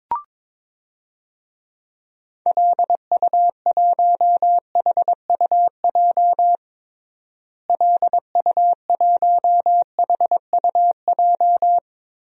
Morse code, heard as a keyed tone, about 22 wpm, 700 Hz. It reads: LU1HUJ LU1HUJ
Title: DXCC-100 Call Signs - 6 Characters - Repeated 3x - 22wpm